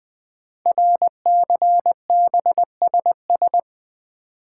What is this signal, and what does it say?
Morse code, heard as a keyed tone, about 20 wpm, 700 Hz. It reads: RCBSS
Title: Code Groups - 5 Letters - 20wpm